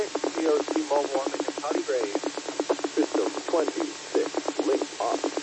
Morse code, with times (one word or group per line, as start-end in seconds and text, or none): none